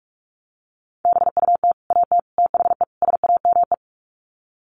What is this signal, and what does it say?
Morse code, heard as a keyed tone, about 44 wpm, 700 Hz. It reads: BUT AT THE SAME
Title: Sets of 4 Words 45wpm